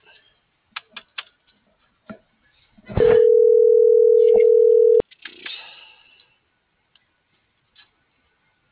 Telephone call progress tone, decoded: ringback tone

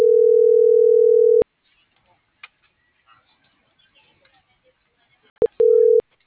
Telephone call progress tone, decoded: ringback tone

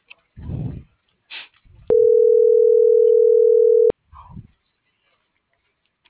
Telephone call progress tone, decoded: ringback tone

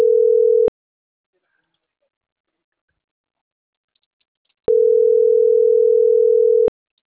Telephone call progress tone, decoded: ringback tone

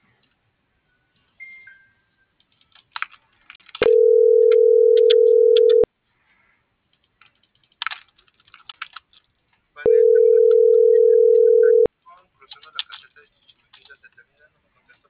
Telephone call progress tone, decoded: ringback tone

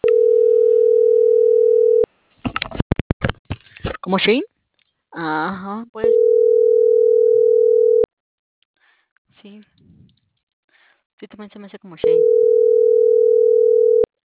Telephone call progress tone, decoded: ringback tone

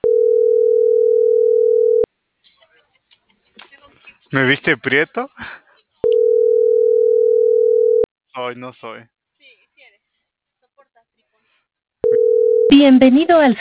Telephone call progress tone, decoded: ringback tone

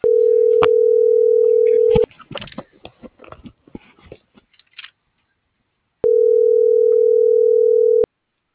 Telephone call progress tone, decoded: ringback tone

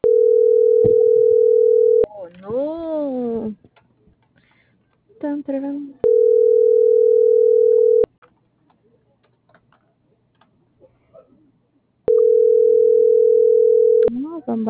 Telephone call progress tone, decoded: ringback tone

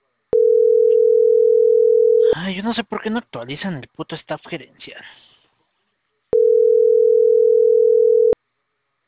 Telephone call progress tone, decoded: ringback tone

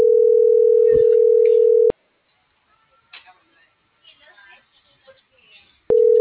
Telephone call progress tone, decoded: ringback tone

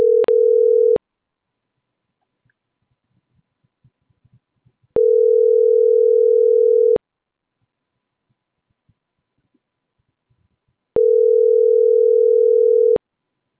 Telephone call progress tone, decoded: ringback tone

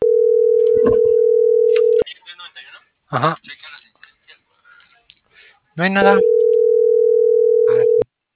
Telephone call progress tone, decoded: ringback tone

